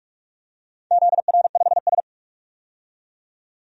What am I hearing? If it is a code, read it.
Morse code, heard as a keyed tone, about 43 wpm, 700 Hz. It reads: ZR5S
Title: DXCC-100 Call Signs - 4 Characters - 45wpm